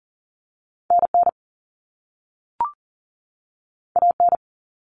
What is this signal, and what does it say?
Morse code, heard as a keyed tone, about 40 wpm, 700 Hz. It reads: NN AN
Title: Common Cut Numbers - 2-Digit 40wpm